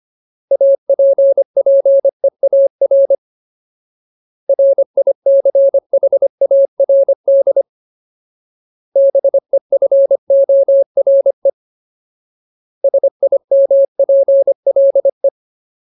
Morse code, heard as a keyed tone, about 25 wpm, 550 Hz. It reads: APPEAR RICHARD BEFORE SIMPLE
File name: Ultimate Word List / 25WPM Ultimate